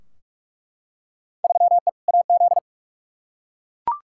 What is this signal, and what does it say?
Morse code, heard as a keyed tone, about 46 wpm, 700 Hz. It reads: 3E AZ
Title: Field Day Exchanges - 45wpm